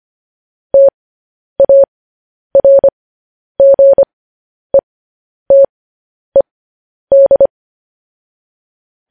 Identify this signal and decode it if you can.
Morse code, heard as a keyed tone, about 25 wpm, spaced as for 10 wpm, 550 Hz. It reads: TARGETED